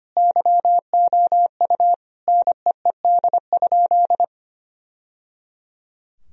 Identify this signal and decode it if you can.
Morse code, heard as a keyed tone, about 25 wpm, 700 Hz. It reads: YOU NEED?
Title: Sentences from Top 1000 Words - Review Part 2 - 25wpm